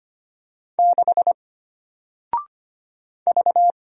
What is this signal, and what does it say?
Morse code, heard as a keyed tone, about 25 wpm, 700 Hz. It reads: 6 V
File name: Single-Letter-Number-25wpm